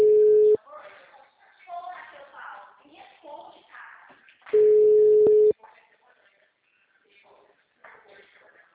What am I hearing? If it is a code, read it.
Ringback tone (Europe, 425 Hz), with speech.
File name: out- -2011- - - .2820